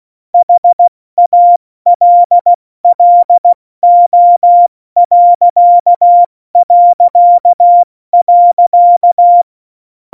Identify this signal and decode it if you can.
Morse code, heard as a keyed tone, about 16 wpm, 700 Hz. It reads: HALLO...
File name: Morsekurs / Witze / Witz 15. August 17, Tempo 70 F80